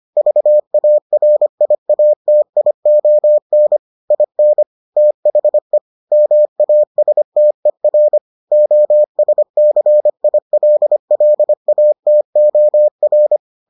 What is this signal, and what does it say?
Morse code, heard as a keyed tone, about 25 wpm, 600 Hz. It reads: VARIATION IN THE MASTER OSCILLATOR